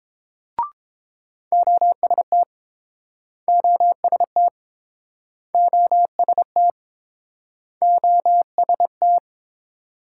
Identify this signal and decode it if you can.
Morse code, heard as a keyed tone, about 31 wpm, 700 Hz. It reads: OST OST OST OST